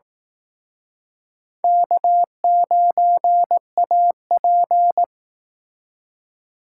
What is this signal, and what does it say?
Morse code, heard as a keyed tone, about 19 wpm, 700 Hz. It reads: K9AP
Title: Speed Racing - USA Call Signs 15wpm